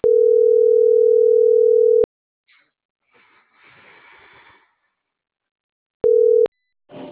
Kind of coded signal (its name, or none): telephone call progress tone